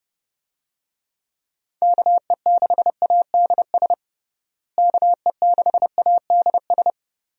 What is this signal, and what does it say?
Morse code, heard as a keyed tone, about 30 wpm, 700 Hz. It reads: KE6ADS KE6ADS
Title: USA Call Signs - 6 Characters - Repeated 2x - 30wpm